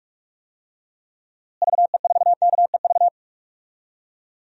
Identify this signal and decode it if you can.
Morse code, heard as a keyed tone, about 41 wpm, 700 Hz. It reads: VE4KEV
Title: DXCC-100 Call Signs - 6 Characters - Repeated 3x - 45wpm